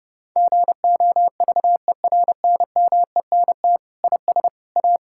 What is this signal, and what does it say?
Morse code, heard as a keyed tone, about 31 wpm, 700 Hz. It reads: GOVERNMENT IS A
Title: Sentences from Top 1000 Words - Review Part 6 - 30wpm